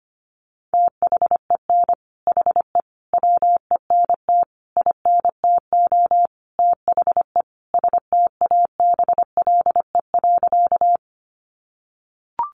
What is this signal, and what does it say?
Morse code, heard as a keyed tone, about 25 wpm, 700 Hz. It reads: THEN HE WENT INTO THE STABLE.
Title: Sentences from Top 1500 Words 25wpm